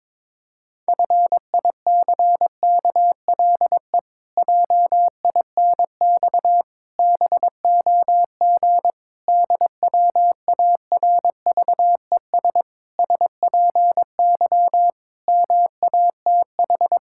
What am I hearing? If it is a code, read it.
Morse code, heard as a keyed tone, about 22 wpm, 700 Hz. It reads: FICKLE JINX BOG DWARVES SPY MATH
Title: Pangram Challenge v2 - 22wpm